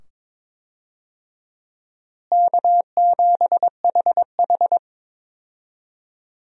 Morse code, heard as a keyed tone, about 22 wpm, 700 Hz. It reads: K7HH
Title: USA Call Signs - 4 Characters - Repeated 2x - 22wpm